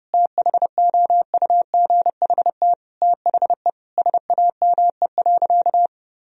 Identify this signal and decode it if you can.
Morse code, heard as a keyed tone, about 30 wpm, 700 Hz. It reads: THOUGHT THE SAME.
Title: Sentences from Top 1000 Words - Review Part 1 - 30wpm